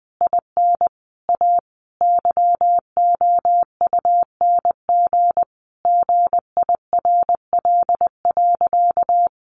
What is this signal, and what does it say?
Morse code, heard as a keyed tone, about 20 wpm, 700 Hz. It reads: IN A YOUNG GIRL.
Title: Sentences from Top 1400 Words 20wpm